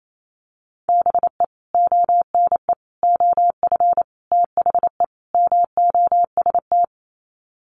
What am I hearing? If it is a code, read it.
Morse code, heard as a keyed tone, about 28 wpm, 700 Hz. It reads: BE ONE OF THE MOST
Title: Sets of 5 Words Encore 28wpm